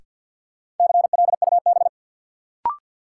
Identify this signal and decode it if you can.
Morse code, heard as a keyed tone, about 50 wpm, 700 Hz. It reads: CLUB
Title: Sentences from Top 1200 Words - New Word Drill Alt - 50wpm